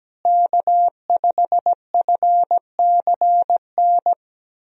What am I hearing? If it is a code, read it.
Morse code, heard as a keyed tone, about 17 wpm, 700 Hz. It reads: K5FCN
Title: USA Call Signs - 5 Characters - Repeated 3x - 17wpm